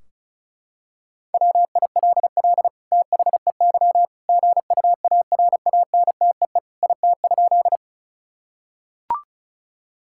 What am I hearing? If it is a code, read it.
Morse code, heard as a keyed tone, about 34 wpm, 700 Hz. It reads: WILL THEY GUARANTEE IT?
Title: Sentences from Top 1600 Words 35wpm